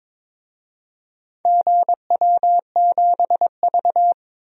Morse code, heard as a keyed tone, about 22 wpm, 700 Hz. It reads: GW7V
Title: DXCC-100 Call Signs - 4 Characters - 22wpm